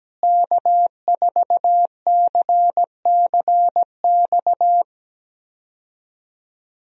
Morse code, heard as a keyed tone, about 17 wpm, 700 Hz. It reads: K4CCX
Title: USA Call Signs - 5 Characters - Repeated 3x - 17wpm